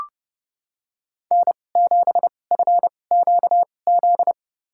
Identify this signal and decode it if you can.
Morse code, heard as a keyed tone, about 30 wpm, spaced as for 22 wpm, 700 Hz. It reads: N7FQZ
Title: USA Call Signs - 5 Characters - 2x Character Spacing - 30wpm